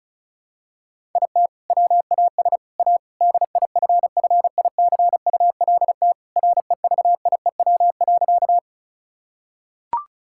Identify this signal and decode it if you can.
Morse code, heard as a keyed tone, about 36 wpm, 690 Hz. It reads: IT WAS A DIFFICULT REVIEW.
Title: Sentences from Top 1000 Words - Review Part 6 - 35wpm